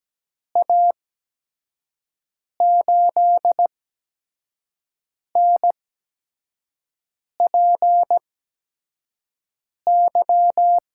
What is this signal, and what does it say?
Morse code, heard as a keyed tone, about 17 wpm, spaced as for 5 wpm, 700 Hz. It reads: A8NPY
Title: USA Call Signs - 6 Characters - 8x Character Spacing - 17wpm